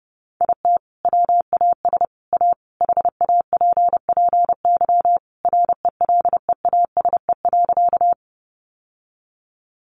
Morse code, heard as a keyed tone, about 30 wpm, 700 Hz. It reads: IT WAS A HAPPY RELEASE.
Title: Sentences from Top 1100 Words 30wpm